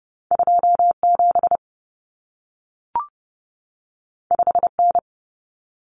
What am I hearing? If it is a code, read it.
Morse code, heard as a keyed tone, about 30 wpm, 700 Hz. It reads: 27 5N
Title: Common Cut Numbers - 2-Digit 30wpm